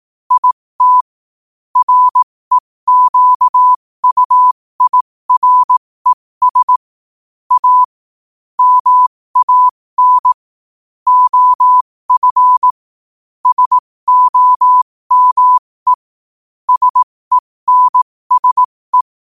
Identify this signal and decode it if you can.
Morse code, heard as a keyed tone, about 18 wpm, spaced as for 15 wpm, 1000 Hz. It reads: IT REQUIRES A MAN OF SOME SENSE
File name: qotdfiles / 15wpm